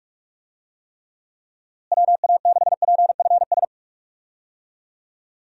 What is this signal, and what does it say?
Morse code, heard as a keyed tone, about 44 wpm, 700 Hz. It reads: WA6PFS
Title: USA Call Signs - 6 Characters - Repeated 2x - 45wpm